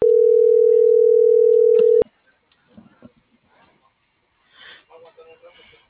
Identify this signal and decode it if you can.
Ringback tone (North America), 440 + 480 Hz.